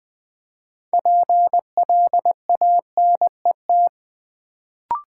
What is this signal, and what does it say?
Morse code, heard as a keyed tone, about 20 wpm, 700 Hz. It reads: PLANET
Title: Build A Word - 3 Letters At A Time - 6-Letter Words - 20wpm